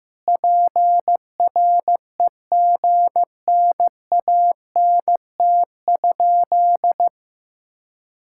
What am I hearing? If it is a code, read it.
Morse code, heard as a keyed tone, about 15 wpm, 700 Hz. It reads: PREGNANT?
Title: Sentences from Top 1000 Words - Review Part 6 - 15wpm